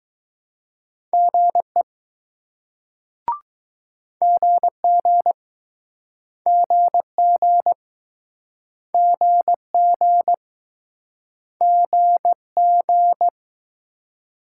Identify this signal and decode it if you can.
Morse code, heard as a keyed tone, about 22 wpm, 700 Hz. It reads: GE GG GG GG GG